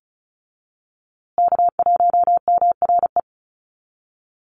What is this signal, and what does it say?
Morse code, heard as a keyed tone, about 34 wpm, 700 Hz. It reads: K1MRE
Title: USA Call Signs Repeated 2x 35wpm